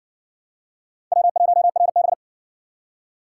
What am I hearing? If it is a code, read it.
Morse code, heard as a keyed tone, about 59 wpm, 700 Hz. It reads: W1RB